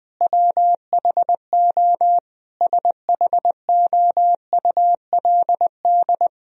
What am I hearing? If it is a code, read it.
Morse code, heard as a keyed tone, about 20 wpm, 700 Hz. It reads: WHO SHOULD
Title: Sentences from Top 1000 Words - Review Part 2 - 20wpm